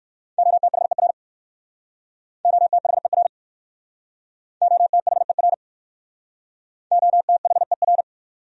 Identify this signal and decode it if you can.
Morse code, heard as a keyed tone, about 57 wpm, 700 Hz. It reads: OTHER OTHER OTHER OTHER